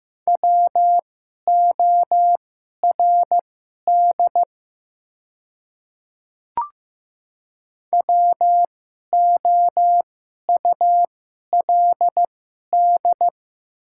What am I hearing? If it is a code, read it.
Morse code, heard as a keyed tone, about 15 wpm, spaced as for 11 wpm, 700 Hz. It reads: WORD WOULD
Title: 2x Character Spacing - Top 100 Words - 15wpm